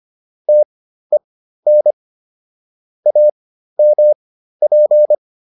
Morse code, heard as a keyed tone, about 25 wpm, spaced as for 13 wpm, 600 Hz. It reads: TEN AMP